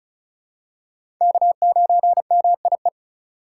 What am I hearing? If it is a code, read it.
Morse code, heard as a keyed tone, about 36 wpm, 700 Hz. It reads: K9MIE
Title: USA Call Signs - 2x Character Spacing - 35wpm